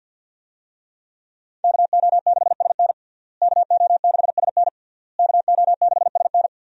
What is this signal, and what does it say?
Morse code, heard as a keyed tone, about 52 wpm, 700 Hz. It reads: KO6SN KO6SN KO6SN
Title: USA Call Signs - 5 Characters - Repeated 3x - 50wpm